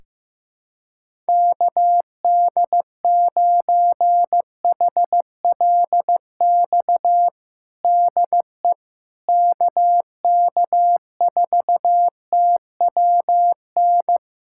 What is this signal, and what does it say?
Morse code, heard as a keyed tone, about 15 wpm, 700 Hz. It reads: KD9HLX DE KK4TWN
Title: USA Callsign DE Callsign - 6 Characters - 15wpm